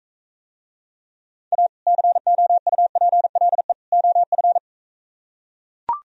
Morse code, heard as a keyed tone, about 42 wpm, 700 Hz. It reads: A COUPLE OF